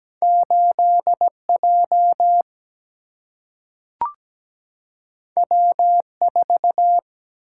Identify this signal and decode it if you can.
Morse code, heard as a keyed tone, about 17 wpm, 700 Hz. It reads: 8J W4
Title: Single-Letter Single-Number 17wpm